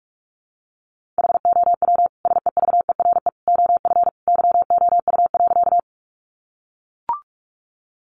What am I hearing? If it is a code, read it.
Morse code, heard as a keyed tone, about 44 wpm, 700 Hz. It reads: HOW SEVERE OF YOU.
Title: Sentences from Top 1400 Words 45wpm